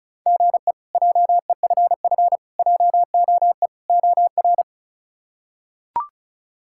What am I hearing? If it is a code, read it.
Morse code, heard as a keyed tone, about 34 wpm, 700 Hz. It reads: GE JEFF JOE OR
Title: K1USN SST QSO Practice - 35wpm